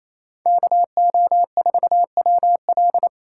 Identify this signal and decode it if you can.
Morse code, heard as a keyed tone, about 28 wpm, 700 Hz. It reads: KO4WL
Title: USA Call Signs - 5 Characters - 2x Character Spacing - 28wpm